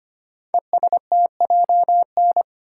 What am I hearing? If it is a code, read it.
Morse code, heard as a keyed tone, about 25 wpm, 700 Hz. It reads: ESTJN
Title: Code Groups - 5 Letter:Numbers - 25wpm